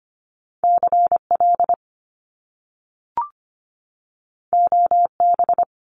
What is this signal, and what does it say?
Morse code, heard as a keyed tone, about 25 wpm, 700 Hz. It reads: CL OB
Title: Top Two Letter Combinations in Words 25wpm